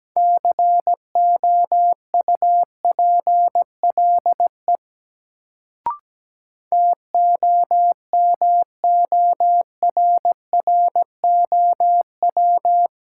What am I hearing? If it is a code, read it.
Morse code, heard as a keyed tone, about 17 wpm, 700 Hz. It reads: COUPLE TOMORROW
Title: Sentences from Top 1100 Words - New Word Drill Alt - 17wpm